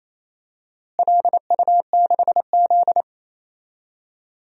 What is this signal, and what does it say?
Morse code, heard as a keyed tone, about 28 wpm, 700 Hz. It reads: LU6Z
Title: DXCC-100 Call Signs - 4 Characters - Repeated 2x - 28wpm